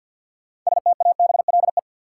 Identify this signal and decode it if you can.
Morse code, heard as a keyed tone, about 52 wpm, 700 Hz. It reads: STABLE
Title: Sentences from Top 1500 Words - New Word Drill - 50wpm